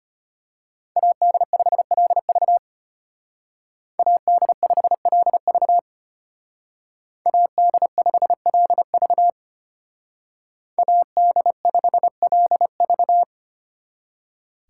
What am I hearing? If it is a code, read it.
Morse code, heard as a keyed tone, about 31 wpm, 700 Hz. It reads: AD5LV AD5LV AD5LV AD5LV
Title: Speed Racing - USA Call Signs 25wpm